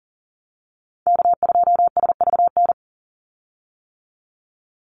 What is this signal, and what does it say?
Morse code, heard as a keyed tone, about 40 wpm, 700 Hz. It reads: K2SVN
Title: USA Call Signs 40wpm